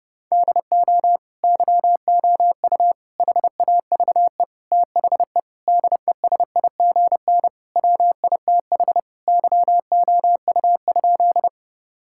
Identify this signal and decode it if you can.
Morse code, heard as a keyed tone, about 32 wpm, 700 Hz. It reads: DO YOU HAVE THE DESIGN WITH YOU?
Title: Sentences from Top 1000 Words - Review Part 5 - 30wpm